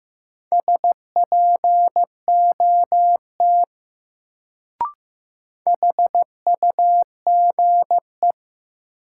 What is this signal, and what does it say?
Morse code, heard as a keyed tone, about 15 wpm, 700 Hz. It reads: SPOT HUGE
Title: Sentences from Top 1100 Words - New Word Drill - 15wpm